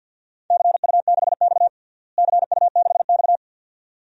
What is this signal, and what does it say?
Morse code, heard as a keyed tone, about 52 wpm, 700 Hz. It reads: CU6X CU6X
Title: DXCC-100 Call Signs - 4 Characters - Repeated 2x - 50wpm